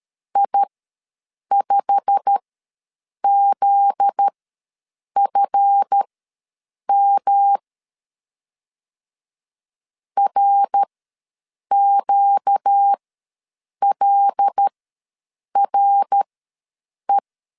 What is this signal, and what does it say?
Morse code, heard as a keyed tone, about 13 wpm, spaced as for 7 wpm, 800 Hz. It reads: I5ZFM RQLRE
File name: LEZIONI / 9 28 ws 12wpm 3354